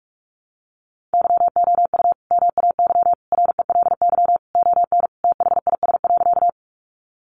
Morse code, heard as a keyed tone, about 44 wpm, 700 Hz. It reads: YOU MAY RELY ON THIS.
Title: Sentences from Top 1200 Words 45wpm